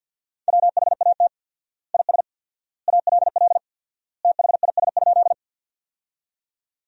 Morse code, heard as a keyed tone, about 48 wpm, 700 Hz. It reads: WHAT IS ALL THIS?
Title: 4x Word-Spacing - Sentences from Top 100 Words - 50wpm